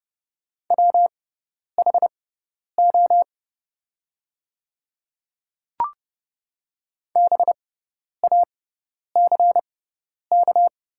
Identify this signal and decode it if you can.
Morse code, heard as a keyed tone, about 30 wpm, spaced as for 10 wpm, 700 Hz. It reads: WHO BACK
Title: 6x Character Spacing - Top 300 Words - 30wpm